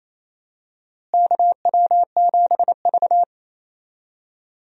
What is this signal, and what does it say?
Morse code, heard as a keyed tone, about 28 wpm, 700 Hz. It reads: KW7V
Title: USA Call Signs - 4 Characters - Repeated 2x - 28wpm